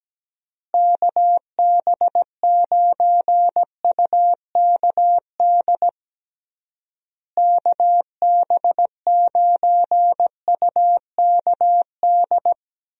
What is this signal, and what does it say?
Morse code, heard as a keyed tone, about 17 wpm, 700 Hz. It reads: KB9UKD KB9UKD
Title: USA Call Signs - 6 Characters - Repeated 3x - 17wpm